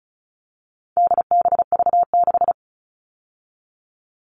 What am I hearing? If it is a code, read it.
Morse code, heard as a keyed tone, about 34 wpm, 700 Hz. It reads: DBV6
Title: All Cut Numbers - 4-Digit 35wpm